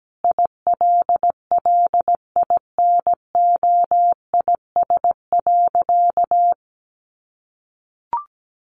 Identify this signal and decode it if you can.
Morse code, heard as a keyed tone, about 17 wpm, 700 Hz. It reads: ILLINOIS.
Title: A Sound Of Thunder 17wpm